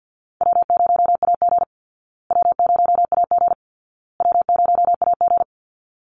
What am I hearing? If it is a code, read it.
Morse code, heard as a keyed tone, about 49 wpm, 700 Hz. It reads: W0AG W0AG W0AG
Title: Call Signs Repeated 2x - 50wpm